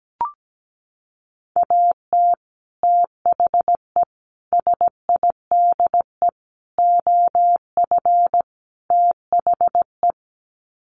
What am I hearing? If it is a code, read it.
Morse code, heard as a keyed tone, about 17 wpm, 700 Hz. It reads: AT THE SIDE OF THE
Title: Sets of 5 Words Encore2 17wpm